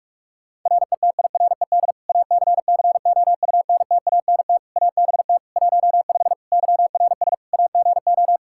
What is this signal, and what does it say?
Morse code, heard as a keyed tone, about 41 wpm, 700 Hz. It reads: RETIRED ACCOUNTANT ABT 15 YRS AGO
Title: Practice QSO - Exchange 3 - More Info - 45wpm